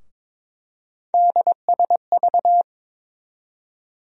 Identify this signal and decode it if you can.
Morse code, heard as a keyed tone, about 22 wpm, 700 Hz. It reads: DSV
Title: USA Call Sign Postfixes - 3 Letters 22wpm